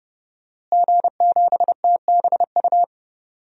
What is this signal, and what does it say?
Morse code, heard as a keyed tone, about 32 wpm, 700 Hz. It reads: G7TBU
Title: DXCC-100 Call Signs - 5 Characters - Repeated 3x - 30wpm